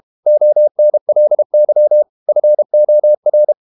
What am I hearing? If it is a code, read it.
Morse code, heard as a keyed tone, about 32 wpm, 600 Hz. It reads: ONLY FOR
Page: Never 31wpm | CW med Gnister